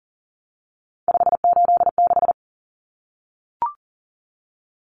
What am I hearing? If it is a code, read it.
Morse code, heard as a keyed tone, about 40 wpm, 700 Hz. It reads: 586
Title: Common Cut Numbers - 3-Digit 40wpm